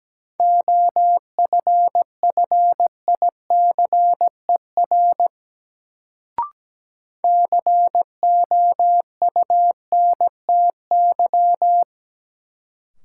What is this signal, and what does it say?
Morse code, heard as a keyed tone, about 17 wpm, 700 Hz. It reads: OFFICER COUNTY
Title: Sentences from Top 1200 Words - New Word Drill Alt - 17wpm